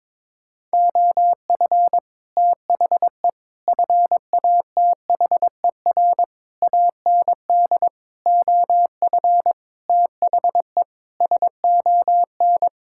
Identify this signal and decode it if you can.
Morse code, heard as a keyed tone, about 22 wpm, 700 Hz. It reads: OF THE FATHER AND OF THE SON
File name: Sets of 9 Words - 22wpm